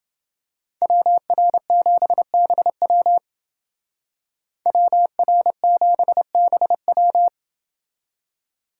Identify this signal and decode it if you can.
Morse code, heard as a keyed tone, about 28 wpm, 700 Hz. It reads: WR7BW WR7BW